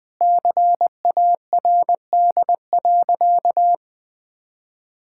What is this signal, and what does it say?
Morse code, heard as a keyed tone, about 20 wpm, 700 Hz. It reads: CARD.
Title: Sentences with 7-Word Phrases - Spoken Beginning - 20wpm